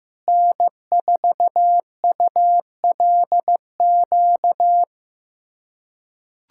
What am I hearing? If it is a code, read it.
Morse code, heard as a keyed tone, about 15 wpm, 700 Hz. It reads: N4ULQ
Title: USA Callsign DE Callsign - 5 Characters - 15wpm